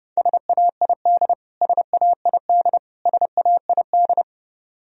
Morse code, heard as a keyed tone, about 30 wpm, 700 Hz. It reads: SAID SAID SAID
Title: Top 100 Words Repeated 30wpm